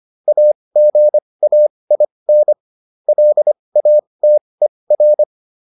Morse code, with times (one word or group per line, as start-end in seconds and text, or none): 0.3-2.5 AGAIN
3.1-5.2 LATER